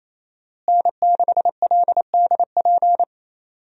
Morse code, heard as a keyed tone, about 28 wpm, 700 Hz. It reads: N6LDP
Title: USA Call Signs - 5 Characters - Repeated 2x - 28wpm